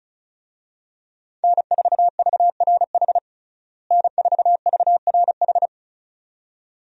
Morse code, heard as a keyed tone, about 36 wpm, 700 Hz. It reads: N4VRH N4VRH
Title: USA Call Signs - 5 Characters - Repeated 2x - 35wpm